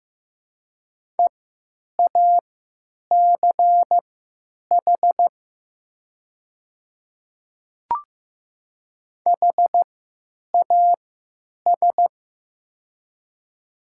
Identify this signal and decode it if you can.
Morse code, heard as a keyed tone, about 15 wpm, spaced as for 9 wpm, 700 Hz. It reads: EACH HAS
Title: 3x Character Spacing - Top 100 Words - 15wpm